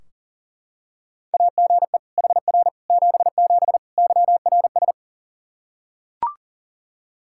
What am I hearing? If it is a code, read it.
Morse code, heard as a keyed tone, about 40 wpm, 700 Hz. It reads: AGE HR 77 YRS
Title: Practice QSO - Exchange 3 - More Info - 40wpm